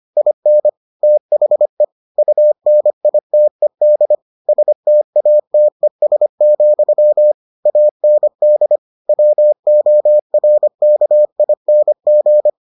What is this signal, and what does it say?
Morse code, heard as a keyed tone, about 25 wpm, 600 Hz. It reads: IN THE UNITED STATES, AND WORKING